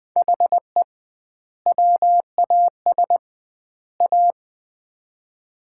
Morse code, heard as a keyed tone, about 20 wpm, 700 Hz. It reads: HE WAS A
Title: 2x Word-Spacing - Sets of 3 Words - 20wpm